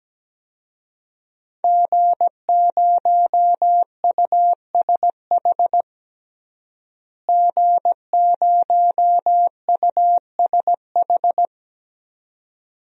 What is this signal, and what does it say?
Morse code, heard as a keyed tone, about 17 wpm, 700 Hz. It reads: G0USH G0USH
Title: DXCC-100 Call Signs - 5 Characters - Repeated 2x - 17wpm